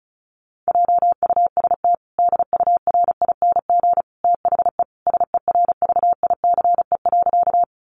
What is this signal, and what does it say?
Morse code, heard as a keyed tone, about 34 wpm, 700 Hz. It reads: JUST DURING THE SERVICE.
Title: Sentences from Top 700 Words 35wpm